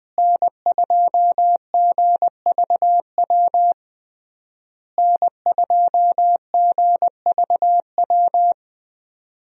Morse code, heard as a keyed tone, about 20 wpm, 700 Hz. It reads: N2GVW N2GVW
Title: USA Call Signs - 5 Characters - Repeated 3x - 20wpm